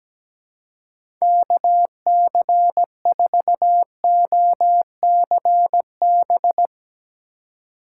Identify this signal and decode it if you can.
Morse code, heard as a keyed tone, about 17 wpm, 700 Hz. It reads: KC4OCB
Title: USA Call Signs - 6 Characters - 2x Character Spacing - 17wpm